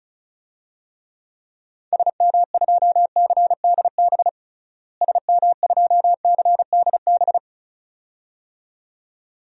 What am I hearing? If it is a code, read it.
Morse code, heard as a keyed tone, about 34 wpm, 700 Hz. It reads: SM2CDB SM2CDB
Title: DXCC-100 Call Signs - 6 Characters - Repeated 2x - 35wpm